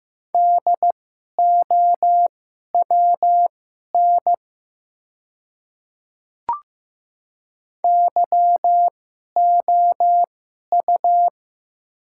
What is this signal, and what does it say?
Morse code, heard as a keyed tone, about 15 wpm, spaced as for 11 wpm, 700 Hz. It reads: DOWN YOU
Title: 2x Character Spacing - Top 100 Words - 15wpm